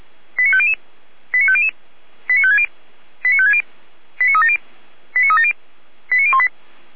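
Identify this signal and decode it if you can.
Five-tone selective call (ZVEI), heard as quick runs of five tones, tones of about 70 ms. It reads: 89500 89500 89560 89560 89370 89370 89018
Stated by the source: ZVEI five-tone call received on SDR.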